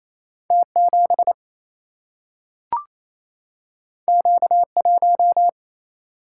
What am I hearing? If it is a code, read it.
Morse code, heard as a keyed tone, about 28 wpm, 700 Hz. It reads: T 7 Q 1
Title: Single-Letter Single-Number 28wpm